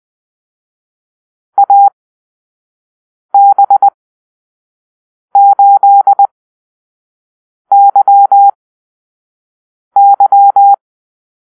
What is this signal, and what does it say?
Morse code, heard as a keyed tone, about 20 wpm, spaced as for 5 wpm, 800 Hz. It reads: AB8YY
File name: arrl / Call-Sign Drill-1